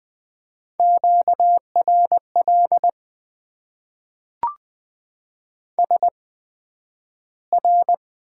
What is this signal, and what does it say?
Morse code, heard as a keyed tone, about 20 wpm, spaced as for 11 wpm, 700 Hz. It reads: QRL SR
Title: Top 90 QSO Elements - 8x Character Spacing - 20wpm